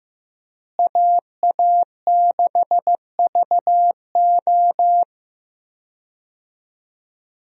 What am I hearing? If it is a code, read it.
Morse code, heard as a keyed tone, about 16 wpm, 700 Hz. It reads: AA6VO
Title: USA Call Signs - 5 Characters - Repeated 2x - 15wpm